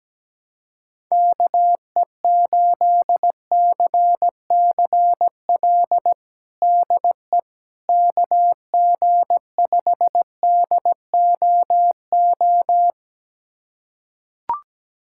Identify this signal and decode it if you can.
Morse code, heard as a keyed tone, about 17 wpm, 700 Hz. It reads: KE8CCL DE KG5DOO
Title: USA Callsign DE Callsign - 6 Characters - 17wpm